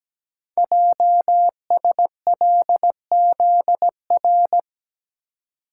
Morse code, heard as a keyed tone, about 17 wpm, 700 Hz. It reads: JSLZR
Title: Code Groups - 5 Letters - 17wpm